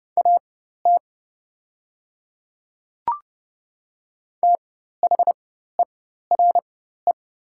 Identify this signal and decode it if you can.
Morse code, heard as a keyed tone, about 30 wpm, spaced as for 14 wpm, 690 Hz. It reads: AT THERE
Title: 4x Character Spacing - Top 100 Words - 30wpm